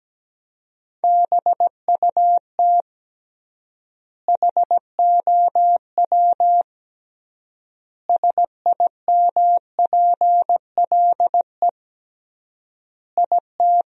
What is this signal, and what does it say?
Morse code, heard as a keyed tone, about 17 wpm, 700 Hz. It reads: BUT HOW SIMPLE IT
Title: 3x Word-Spacing - Sentences from Top 400 Words - 17wpm